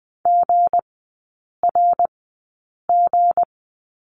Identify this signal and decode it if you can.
Morse code, heard as a keyed tone, about 20 wpm, spaced as for 8 wpm, 690 Hz. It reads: GRG
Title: 159 Lesson 26e - Letter g - Three Characters 20wpm